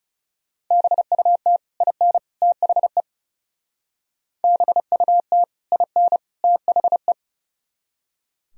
Morse code, heard as a keyed tone, about 39 wpm, 700 Hz. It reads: BUT IN THE BUT IN THE